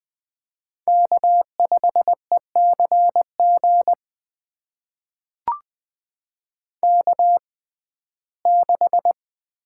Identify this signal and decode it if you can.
Morse code, heard as a keyed tone, about 20 wpm, spaced as for 15 wpm, 700 Hz. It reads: K5ECG K6
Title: USA Call Signs - 5 Characters - 6x Character Spacing - 20wpm